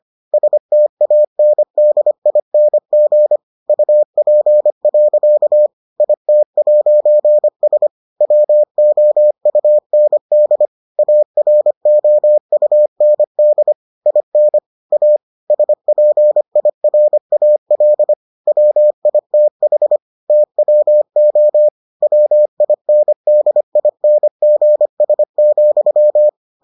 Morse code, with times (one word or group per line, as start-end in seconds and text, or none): 0.3-3.4 STANDING
3.7-5.7 UP.
6.0-7.9 IT'S
8.2-10.7 WOUND
11.0-13.7 AROUND
14.1-14.6 IN
14.9-15.2 A
15.5-18.1 SPIRAL
18.5-20.0 WITH
20.3-21.7 TWO
22.0-26.3 WINDINGS,